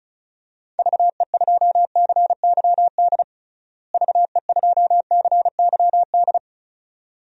Morse code, heard as a keyed tone, about 36 wpm, 700 Hz. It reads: VE2CYD VE2CYD
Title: DXCC-100 Call Signs - 6 Characters - Repeated 2x - 35wpm